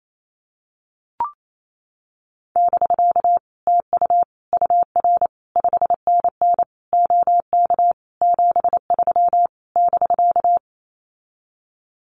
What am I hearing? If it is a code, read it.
Morse code, heard as a keyed tone, about 28 wpm, 700 Hz. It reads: <BK> TU UR 5NN OK 73 <BK>
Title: Common QSO Phrases Encore 28wpm